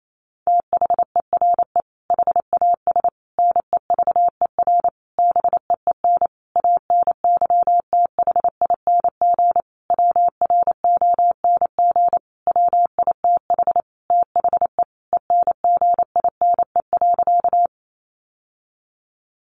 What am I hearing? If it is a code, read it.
Morse code, heard as a keyed tone, about 28 wpm, 700 Hz. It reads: THERE HAS NEVER BEEN ANYTHING WRONG WITH THE ENGINE.
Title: Sentences from Top 1500 Words 28wpm